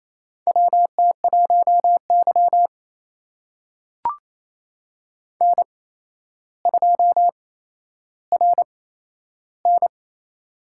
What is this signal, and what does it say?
Morse code, heard as a keyed tone, about 28 wpm, spaced as for 12 wpm, 700 Hz. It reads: WT1Y N2RN
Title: USA Call Signs - 4 Characters - 8x Character Spacing - 28wpm